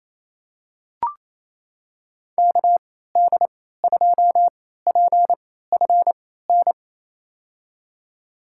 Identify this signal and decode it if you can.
Morse code, heard as a keyed tone, about 28 wpm, spaced as for 16 wpm, 700 Hz. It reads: KD2PFN
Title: USA Call Signs - 3x Character Spacing - 28wpm